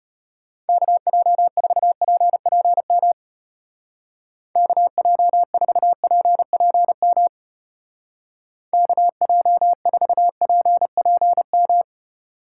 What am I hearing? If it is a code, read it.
Morse code, heard as a keyed tone, about 31 wpm, 700 Hz. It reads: KJ4PPM KJ4PPM KJ4PPM